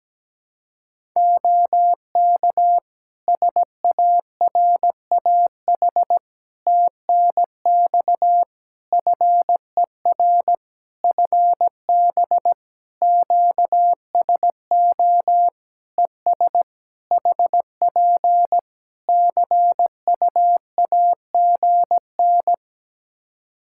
Morse code, heard as a keyed tone, about 17 wpm, 700 Hz. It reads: OK SARAH TNX FER FB QSO ES HP CUAGN
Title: Practice QSO - Exchange 4 - Ending - 17wpm